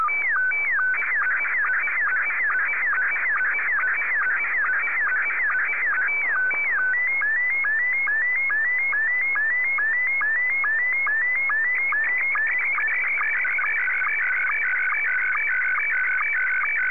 SSTV (slow-scan television), Scottie 1.